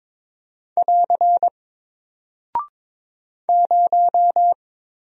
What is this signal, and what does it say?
Morse code, heard as a keyed tone, about 22 wpm, 700 Hz. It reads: <AR> 0
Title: Single Character (Letter:Number:Punctuation:Prosign) 22wpm